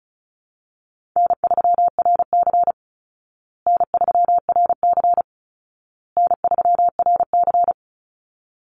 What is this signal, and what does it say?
Morse code, heard as a keyed tone, about 34 wpm, 700 Hz. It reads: N3RC N3RC N3RC
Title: Call Signs Repeated 2x - 35wpm